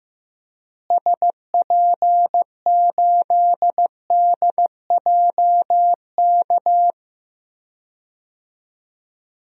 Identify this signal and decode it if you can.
Morse code, heard as a keyed tone, about 15 wpm, 700 Hz. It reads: SP8DJK
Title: DXCC-100 Call Signs - 6 Characters - Repeated 3x - 15wpm